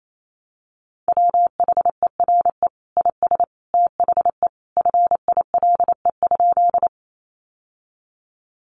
Morse code, heard as a keyed tone, about 28 wpm, 700 Hz. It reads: WHERE IS THE FILE?
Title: Sentences from Top 1200 Words 28wpm